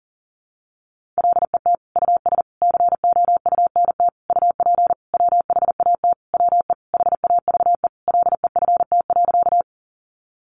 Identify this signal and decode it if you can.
Morse code, heard as a keyed tone, about 40 wpm, 700 Hz. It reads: LET US COUNT UP WHAT WE HAVE LEFT.
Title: Sentences from Top 1500 Words 40wpm